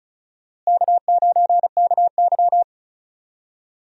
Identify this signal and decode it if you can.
Morse code, heard as a keyed tone, about 36 wpm, 700 Hz. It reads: K9KY
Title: USA Call Signs - 4 Characters - Repeated 2x - 35wpm